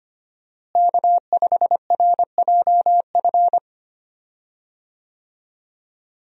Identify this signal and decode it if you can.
Morse code, heard as a keyed tone, about 25 wpm, 700 Hz. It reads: K5RJF
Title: USA Call Signs - 5 Characters - Repeated 2x - 25wpm